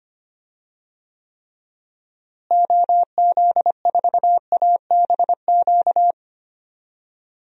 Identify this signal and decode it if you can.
Morse code, heard as a keyed tone, about 25 wpm, 700 Hz. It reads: OZ4ABQ
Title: DXCC-100 Call Signs - 6 Characters - Repeated 2x - 25wpm